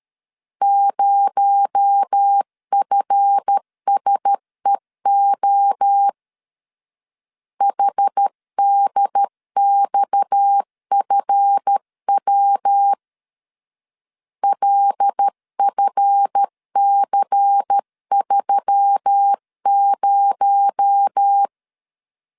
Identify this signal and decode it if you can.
Morse code, heard as a keyed tone, about 13 wpm, 800 Hz. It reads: 0FSEO HDXFW LFC30
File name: LEZIONI / 7 GF9Z / 7 GF 12wpm 2122